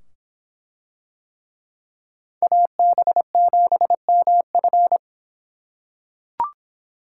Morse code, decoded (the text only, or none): AB7MF